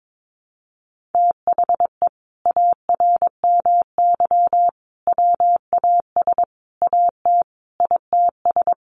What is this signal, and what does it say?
Morse code, heard as a keyed tone, about 22 wpm, 700 Hz. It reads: THE ARMY WAS AT ITS
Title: Sentences from Top 1300 Words 22wpm